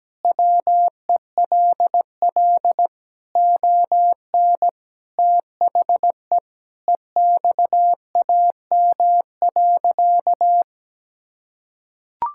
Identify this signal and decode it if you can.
Morse code, heard as a keyed tone, about 17 wpm, 700 Hz. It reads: WELL ON THE EXAM.
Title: Sentences from Top 1000 Words - Review Part 5 - 17wpm